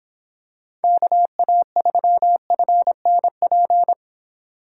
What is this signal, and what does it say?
Morse code, heard as a keyed tone, about 26 wpm, 700 Hz. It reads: KA3FNP